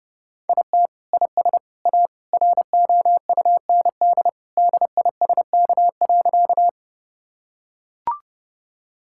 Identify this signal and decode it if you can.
Morse code, heard as a keyed tone, about 29 wpm, 700 Hz. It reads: IT IS A ROUND DISK.
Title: Sentences from Top 1000 Words - Review Part 6 - 30wpm